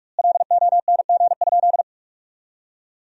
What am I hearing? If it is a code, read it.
Morse code, heard as a keyed tone, about 41 wpm, 700 Hz. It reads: LONG?
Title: 2x Word-Spacing - Sentences from Top 200 Words - 45wpm